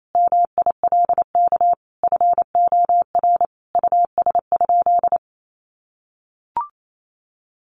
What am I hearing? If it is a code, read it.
Morse code, heard as a keyed tone, about 28 wpm, 700 Hz. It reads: MILK FOR US?
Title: Sentences from Top 1500 Words 28wpm